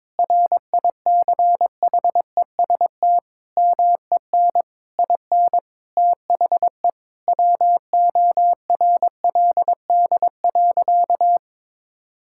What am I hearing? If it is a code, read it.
Morse code, heard as a keyed tone, about 22 wpm, 700 Hz. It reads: RICHEST MEN IN THE WORLD.
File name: practice / 22wpm / Sentences with 8-Word Phrases - 22wpm